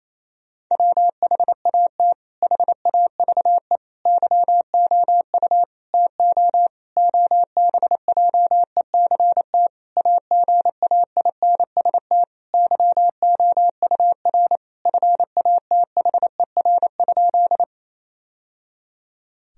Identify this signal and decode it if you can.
Morse code, heard as a keyed tone, about 28 wpm, 700 Hz. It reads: WHAT HAVE YOU TO OBJECT AGAINST YOUR FATHER?
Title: Sentences from Top 1000 Words - Review Part 3 - 28wpm